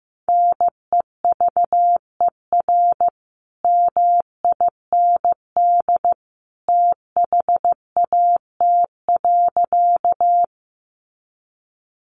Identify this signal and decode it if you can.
Morse code, heard as a keyed tone, about 15 wpm, 700 Hz. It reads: NEVER MIND THAT.
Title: Sentences from Top 500 Words 15wpm